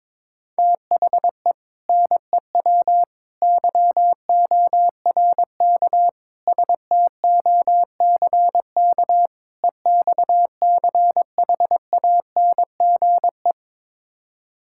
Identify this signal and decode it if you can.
Morse code, heard as a keyed tone, about 22 wpm, 700 Hz. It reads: THE NEW YORK STOCK EXCHANGE
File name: practice / 22wpm / Build A Phrase - 5-Word Phrases - 22wpm